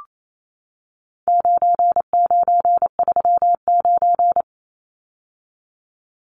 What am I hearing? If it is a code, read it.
Morse code, heard as a keyed tone, about 28 wpm, 700 Hz. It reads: 9939
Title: Numbers - 4-Digit 28wpm